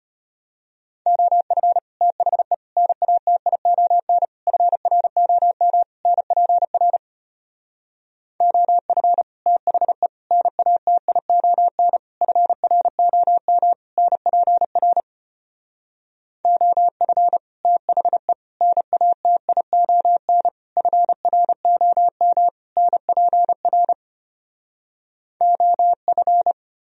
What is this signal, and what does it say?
Morse code, heard as a keyed tone, about 33 wpm, 700 Hz. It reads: OF THE NATION FROM NPR OF THE NATION FROM NPR OF THE NATION FROM NPR OF